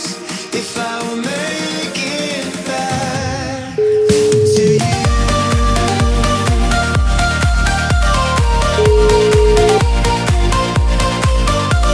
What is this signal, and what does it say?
Ringback tone (Europe, 425 Hz), with noise.